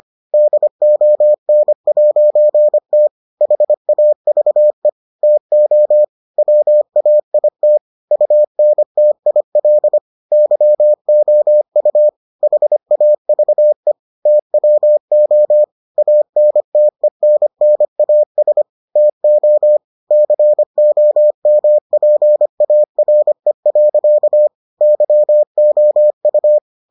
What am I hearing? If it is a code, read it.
Morse code, heard as a keyed tone, about 25 wpm, 600 Hz. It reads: DON'T HAVE TO WAIT UNTIL YOU HAVE TWO ANTENNAS TO COMPARE. YOU